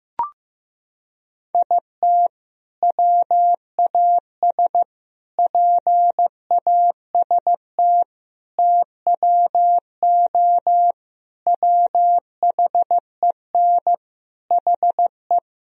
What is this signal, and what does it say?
Morse code, heard as a keyed tone, about 15 wpm, 700 Hz. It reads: IT WAS PAST TWO WHEN HE
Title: Sentences from Top 1000 Words - Review Part 3 - 15wpm